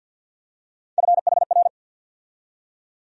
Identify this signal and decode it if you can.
Morse code, heard as a keyed tone, about 51 wpm, 700 Hz. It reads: FHR
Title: USA Call Sign Postfixes - 3 Letters 50wpm